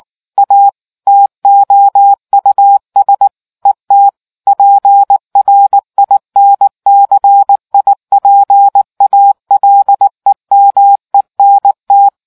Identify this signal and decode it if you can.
Morse code, heard as a keyed tone, about 19 wpm, 800 Hz. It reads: A TOUS ET PRINCIPALEMENT